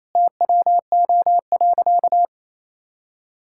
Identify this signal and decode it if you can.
Morse code, heard as a keyed tone, about 28 wpm, 700 Hz. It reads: TWO.
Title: Sentences from Top 1000 Words - Review Part 5 - 28wpm